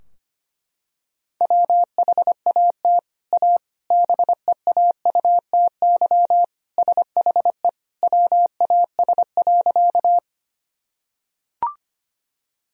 Morse code, heard as a keyed tone, about 25 wpm, 700 Hz. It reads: WHAT A BEAUTY SHE WAS.
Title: Sentences from Top 1000 Words - Review Part 2 - 25wpm